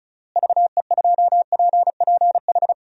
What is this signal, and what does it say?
Morse code, heard as a keyed tone, about 36 wpm, 700 Hz. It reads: VE2PPH
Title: DXCC-100 Call Signs - 6 Characters - Repeated 3x - 35wpm